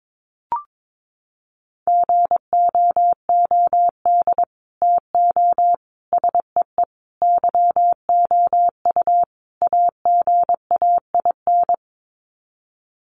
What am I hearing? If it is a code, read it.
Morse code, heard as a keyed tone, about 22 wpm, 700 Hz. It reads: GOOD TO SEE YOU AGAIN
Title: Sets of 5 Words Encore2 22wpm